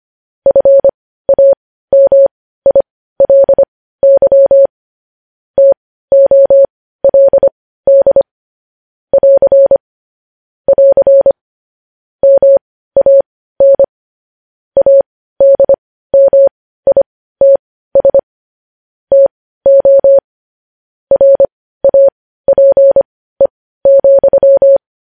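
Morse code, heard as a keyed tone, about 25 wpm, spaced as for 15 wpm, 550 Hz. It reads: FAMILY TOLD <AR> <AR> MAN ADMITS TO RAPE,
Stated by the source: Morse code news headlines sent at 15, opening with vvv.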